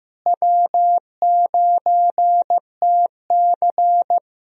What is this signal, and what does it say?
Morse code, heard as a keyed tone, about 15 wpm, 700 Hz. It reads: W9TC
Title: USA Call Signs - 4 Characters - Repeated 2x - 15wpm